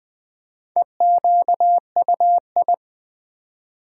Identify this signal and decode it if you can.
Morse code, heard as a keyed tone, about 20 wpm, 700 Hz. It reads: EQUI
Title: Build A Word - 1 Letter At A Time - 5-Letter Words - 20wpm